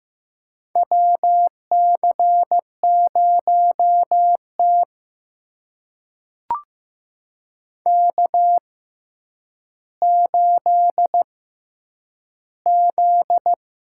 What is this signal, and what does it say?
Morse code, heard as a keyed tone, about 15 wpm, spaced as for 9 wpm, 700 Hz. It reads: WC0T K8Z
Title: USA Call Signs - 4 Characters - 6x Character Spacing - 15wpm